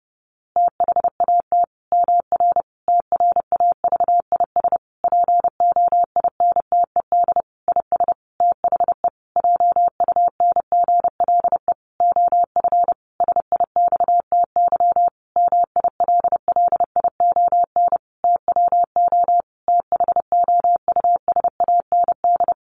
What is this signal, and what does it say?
Morse code, heard as a keyed tone, about 30 wpm, 700 Hz. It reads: THAT MR TRAVIS POINTED IS THE JUNGLE OF SIXTY MILLION TWO THOUSAND
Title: A Sound Of Thunder 30wpm